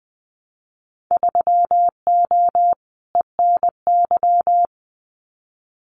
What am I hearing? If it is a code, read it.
Morse code, heard as a keyed tone, about 20 wpm, 700 Hz. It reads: 3O ENY
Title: Winter Field Day Exchanges Encore 20wpm